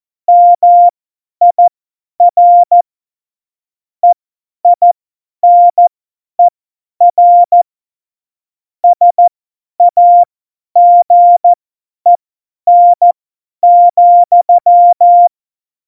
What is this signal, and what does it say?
Morse code, heard as a keyed tone, about 14 wpm, spaced as for 10 wpm, 700 Hz. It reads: MIR EINER SAGEN,
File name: Morsekurs / Witze / Witz 28. Mai 2019, 50 F70